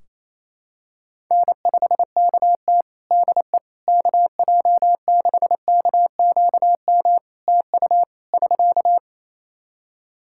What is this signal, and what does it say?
Morse code, heard as a keyed tone, about 28 wpm, 700 Hz. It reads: N5KT DE KJ6KQM TU <SK>
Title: Practice QSO - Exchange 4 - Ending - 28wpm